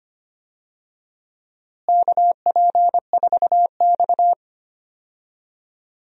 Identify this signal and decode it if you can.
Morse code, heard as a keyed tone, about 25 wpm, 700 Hz. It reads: KP4X
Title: DXCC-100 Call Signs - 4 Characters - Repeated 3x - 25wpm